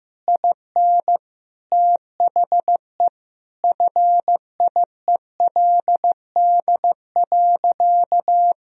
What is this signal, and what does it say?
Morse code, heard as a keyed tone, about 15 wpm, 700 Hz. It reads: IN THE FIELD.
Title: Sentences from Top 1000 Words - Review Part 2 - 15wpm